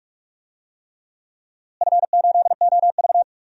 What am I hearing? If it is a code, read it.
Morse code, heard as a keyed tone, about 46 wpm, 700 Hz. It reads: F8OV
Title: DXCC-100 Call Signs - 4 Characters - Repeated 3x - 45wpm